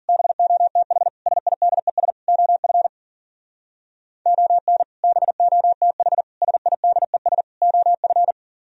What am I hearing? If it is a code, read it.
Morse code, heard as a keyed tone, about 49 wpm, 700 Hz. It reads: BOTH SIDES OF ON BOTH SIDES OF
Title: Speed-Racing - Sets of 4 Words 40wpm